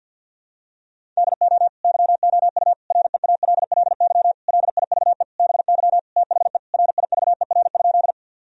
Morse code, heard as a keyed tone, about 52 wpm, 700 Hz. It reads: DO YOU REALLY LIVE BY THE RIVER?
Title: Sentences from Top 1000 Words - Review Part 1 - 50wpm